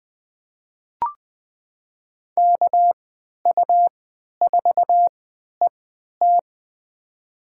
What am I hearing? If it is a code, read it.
Morse code, heard as a keyed tone, about 21 wpm, spaced as for 12 wpm, 700 Hz. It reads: KU4ET
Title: USA Call Signs - 5 Characters - 3x Character Spacing - 20wpm